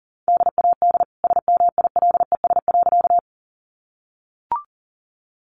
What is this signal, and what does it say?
Morse code, heard as a keyed tone, about 40 wpm, 700 Hz. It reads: DAD SMILES.
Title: Easy Phrases 40wpm